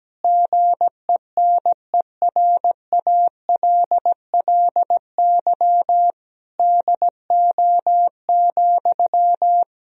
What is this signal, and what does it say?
Morse code, heard as a keyed tone, about 17 wpm, 700 Hz. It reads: GENERALLY DO,
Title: Sentences from Top 1000 Words - Review Part 4 - 17wpm